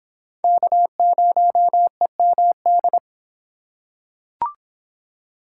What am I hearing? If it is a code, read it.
Morse code, heard as a keyed tone, about 24 wpm, 700 Hz. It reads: K0EMD